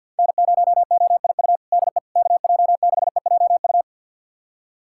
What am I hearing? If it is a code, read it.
Morse code, heard as a keyed tone, about 49 wpm, 700 Hz. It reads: N0OIU DE KJ6EJU
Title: USA Callsign DE Callsign 50wpm